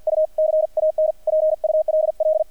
Morse code, heard as a keyed tone, about 50 wpm, 620 Hz, with radio noise. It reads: U CAN PULL
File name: archived / CRSnet